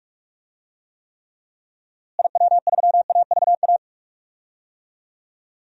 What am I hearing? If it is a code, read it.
Morse code, heard as a keyed tone, about 48 wpm, 700 Hz. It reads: IW3AVA
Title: DXCC-100 Call Signs - 6 Characters - Repeated 2x - 45wpm